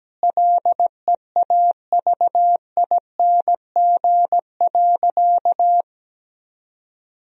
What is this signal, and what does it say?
Morse code, heard as a keyed tone, about 17 wpm, 700 Hz. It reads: LEAVING.
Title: Sentences from Top 1000 Words - Review Part 4 - 17wpm